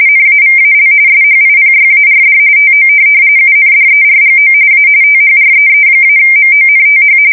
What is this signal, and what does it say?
RTTY, about 45.45 baud. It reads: NAVAL SHIPYARD TO UNDERGO A THREE-MONTH
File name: fsk170 / USS-Missouri